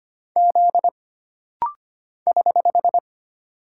Morse code, heard as a keyed tone, about 25 wpm, 700 Hz. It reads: Z <HH>
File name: Single Character (Letter:Number:Punctuation:Prosign) - Rapid-Fire - 25wpm